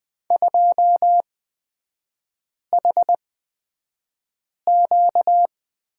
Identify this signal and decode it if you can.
Morse code, heard as a keyed tone, about 20 wpm, spaced as for 5 wpm, 700 Hz. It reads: 2HQ
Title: 255 Lesson 41e - Period - Three Characters 20wpm